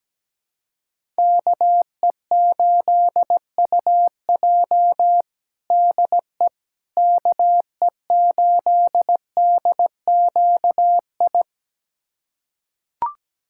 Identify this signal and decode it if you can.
Morse code, heard as a keyed tone, about 17 wpm, 700 Hz. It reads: KE8UJ DE KE8DQI
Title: USA Callsign DE Callsign 17wpm